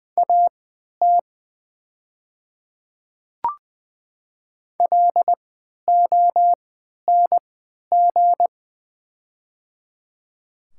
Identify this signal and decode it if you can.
Morse code, heard as a keyed tone, about 20 wpm, spaced as for 11 wpm, 700 Hz. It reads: AT LONG
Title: 3x Character Spacing - Top 200 Words - 20wpm